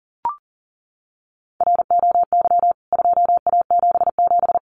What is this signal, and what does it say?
Morse code, heard as a keyed tone, about 37 wpm, 700 Hz. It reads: ROY 2A77
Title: CWT Exchange 40wpm